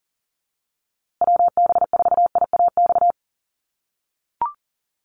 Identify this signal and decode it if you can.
Morse code, heard as a keyed tone, about 40 wpm, 700 Hz. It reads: WB4IAX
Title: USA Call Signs Repeated 3x 40wpm